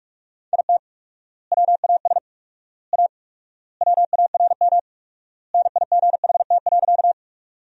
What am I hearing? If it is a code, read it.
Morse code, heard as a keyed tone, about 41 wpm, 700 Hz. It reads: IT WAS A WARM NIGHT.
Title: 4x Word-Spacing - Sentences from Top 500 Words - 45wpm